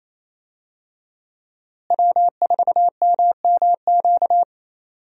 Morse code, heard as a keyed tone, about 28 wpm, 700 Hz. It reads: W4MMQ
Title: USA Call Signs - 5 Characters - Repeated 3x - 28wpm